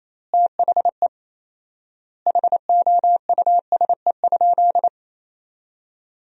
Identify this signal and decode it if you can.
Morse code, heard as a keyed tone, about 27 wpm, 700 Hz. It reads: THE HOUSE?
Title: 4x Word-Spacing - Sentences from Top 200 Words - 28wpm